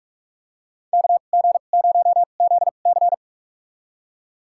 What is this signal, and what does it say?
Morse code, heard as a keyed tone, about 46 wpm, spaced as for 33 wpm, 700 Hz. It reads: KG0ZC
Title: USA Call Signs - 5 Characters - 2x Character Spacing - 45wpm